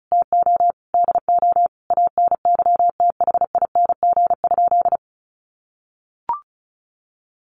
Morse code, heard as a keyed tone, about 34 wpm, 700 Hz. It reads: TO DO ANYTHING?
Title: Sentences from Top 1000 Words 35wpm